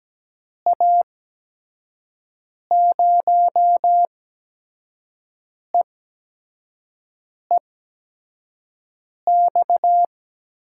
Morse code, heard as a keyed tone, about 17 wpm, spaced as for 5 wpm, 700 Hz. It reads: A0EEX
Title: USA Call Signs - 8x Character Spacing - 17wpm